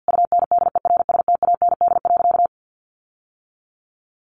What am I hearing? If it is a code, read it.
Morse code, heard as a keyed tone, about 50 wpm, 700 Hz. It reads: UNDERSTAND.
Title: Sentences from Top 1100 Words 50wpm